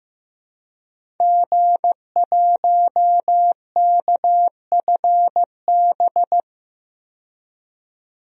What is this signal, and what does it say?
Morse code, heard as a keyed tone, about 15 wpm, 700 Hz. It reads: G1KFB
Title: DXCC-100 Call Signs - 5 Characters - 15wpm